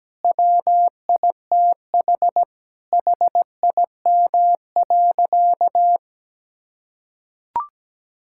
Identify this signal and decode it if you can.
Morse code, heard as a keyed tone, about 17 wpm, 700 Hz. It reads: WITH HIM.
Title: Sentences from Top 1000 Words - Review Part 5 - 17wpm